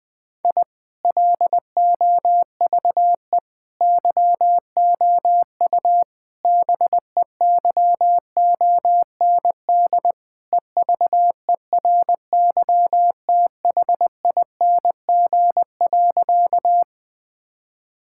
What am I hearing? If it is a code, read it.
Morse code, heard as a keyed tone, about 20 wpm, 700 Hz. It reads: I LOVE YOU BEYOND EVERYTHING.
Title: Sentences from Top 1000 Words - Review Part 6 - 20wpm